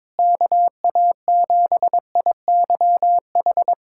Morse code, heard as a keyed tone, about 22 wpm, 700 Hz. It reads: KA7IYH
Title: USA Callsign DE Callsign - 6 Characters - 22wpm